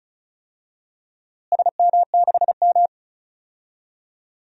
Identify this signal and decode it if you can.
Morse code, heard as a keyed tone, about 36 wpm, 700 Hz. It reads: SM6M
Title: DXCC-100 Call Signs - 4 Characters - Repeated 2x - 35wpm